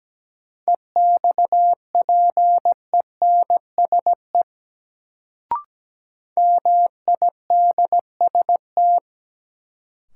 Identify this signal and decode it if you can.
Morse code, heard as a keyed tone, about 17 wpm, 700 Hz. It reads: EXPENSE MIDST
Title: Sets of 4 Words - New Word Drill - 17wpm